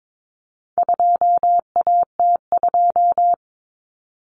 Morse code, heard as a keyed tone, about 22 wpm, 700 Hz. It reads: 2AT2
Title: Common Cut Numbers - 4-Digit 22wpm